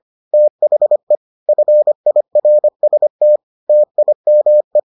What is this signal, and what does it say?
Morse code, heard as a keyed tone, about 25 wpm, 600 Hz. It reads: THE FIRST TIME